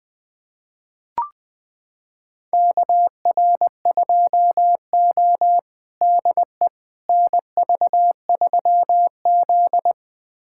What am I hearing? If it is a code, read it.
Morse code, heard as a keyed tone, about 20 wpm, 700 Hz. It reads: KR2O DE NV3Z
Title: USA Callsign DE Callsign - 4 Characters - 20wpm